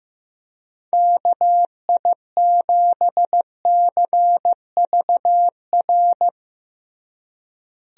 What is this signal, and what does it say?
Morse code, heard as a keyed tone, about 15 wpm, 700 Hz. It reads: KI7CVR
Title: USA Call Signs - 6 Characters - Repeated 2x - 15wpm